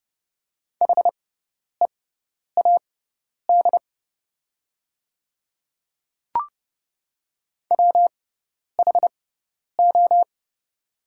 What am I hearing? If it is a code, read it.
Morse code, heard as a keyed tone, about 30 wpm, spaced as for 10 wpm, 700 Hz. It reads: HEAD WHO
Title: 6x Character Spacing - Top 300 Words - 30wpm